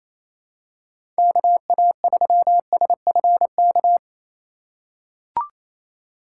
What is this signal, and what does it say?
Morse code, heard as a keyed tone, about 28 wpm, 700 Hz. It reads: KA3SFK
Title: USA Call Signs - 4x Character Spacing - 28wpm